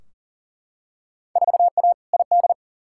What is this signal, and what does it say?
Morse code, heard as a keyed tone, about 40 wpm, 700 Hz. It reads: 4A ID
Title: Field Day Exchanges - 40wpm